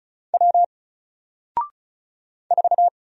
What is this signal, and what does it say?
Morse code, heard as a keyed tone, about 36 wpm, 700 Hz. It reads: W 4
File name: Single-Letter-Number-35wpm